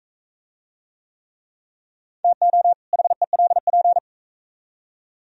Morse code, heard as a keyed tone, about 44 wpm, 700 Hz. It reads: TO HELP